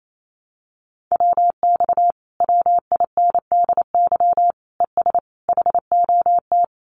Morse code, heard as a keyed tone, about 28 wpm, 700 Hz. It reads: WX WINDY ES HOT
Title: Common QSO Phrases Encore 28wpm